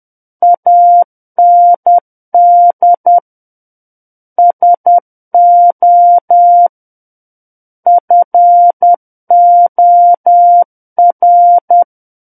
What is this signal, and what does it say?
Morse code, heard as a keyed tone, about 10 wpm, 700 Hz. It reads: AND SO FOR
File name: Book One / 01-Chapter 1 - 10 WPM